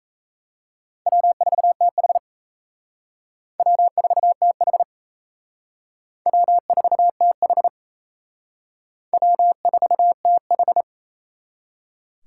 Morse code, heard as a keyed tone, about 36 wpm, 700 Hz. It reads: W4TH W4TH W4TH W4TH